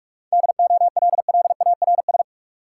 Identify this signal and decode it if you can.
Morse code, heard as a keyed tone, about 41 wpm, 700 Hz. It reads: DOLLARS
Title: Sets of 5 Words - New Word Drill - 45wpm